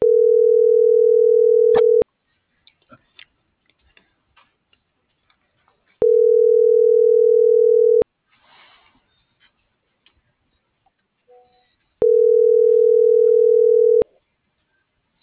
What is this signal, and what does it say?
Ringback tone (North America), 440 + 480 Hz.